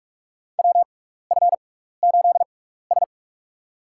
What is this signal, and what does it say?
Morse code, heard as a keyed tone, about 41 wpm, spaced as for 15 wpm, 700 Hz. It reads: WF8S
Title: USA Call Signs - 4 Characters - 6x Character Spacing - 45wpm